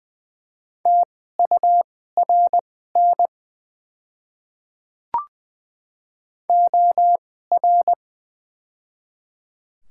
Morse code, heard as a keyed tone, about 20 wpm, spaced as for 15 wpm, 690 Hz. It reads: TURN OR
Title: 2x Character Spacing - Top 200 Words - 20wpm